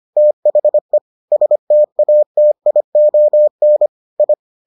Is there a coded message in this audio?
Yes: Morse code from 0.2 to 4.3 s, heard as a keyed tone, about 25 wpm, 600 Hz. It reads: THE STATION I